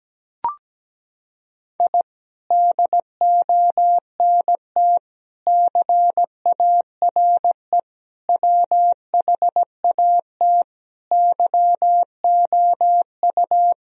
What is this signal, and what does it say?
Morse code, heard as a keyed tone, about 17 wpm, 700 Hz. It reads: I DONT CARE WHAT YOU
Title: Sentences from Top 1000 Words - Review Part 1 - 17wpm